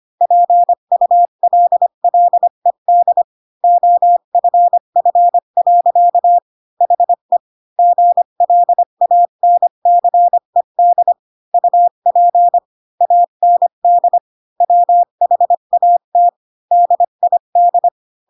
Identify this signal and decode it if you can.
Morse code, heard as a keyed tone, about 25 wpm, 700 Hz. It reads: PULLED OFF. HE GLANCED UP AND WHAT DID
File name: Pinocchio / Pinocchio ch3 20 25 ik1hge